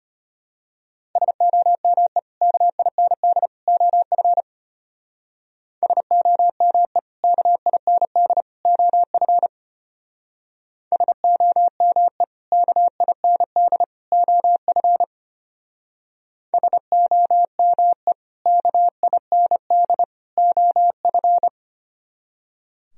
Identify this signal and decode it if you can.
Morse code, heard as a keyed tone, about 31 wpm, 700 Hz. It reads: SOME KIND OF SOME KIND OF SOME KIND OF SOME KIND OF
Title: Speed-Racing - Sets of 3 Words 25wpm